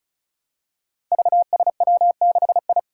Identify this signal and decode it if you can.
Morse code, heard as a keyed tone, about 36 wpm, 700 Hz. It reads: VSW6I
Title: Five Challenging Letter-Number Combinations 35wpm